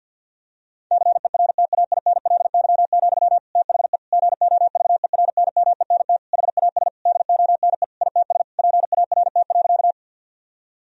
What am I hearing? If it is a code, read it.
Morse code, heard as a keyed tone, about 50 wpm, 700 Hz. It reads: CERTAINLY, THE GOVERNMENT HAS DONE ITS PART.
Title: Sentences from Top 1000 Words - Review Part 3 - 50wpm